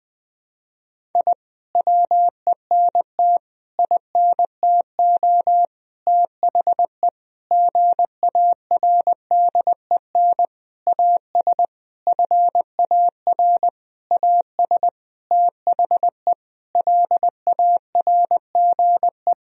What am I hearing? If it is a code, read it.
Morse code, heard as a keyed tone, about 20 wpm, 700 Hz. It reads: I WENT INTO THE GARDEN AS FAR AS THE LARGE
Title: Sentences from Top 1000 Words - Review Part 4 - 20wpm